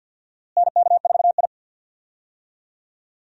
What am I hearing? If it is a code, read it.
Morse code, heard as a keyed tone, about 46 wpm, 700 Hz. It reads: NK4I
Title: USA Call Signs - 4 Characters - Repeated 2x - 50wpm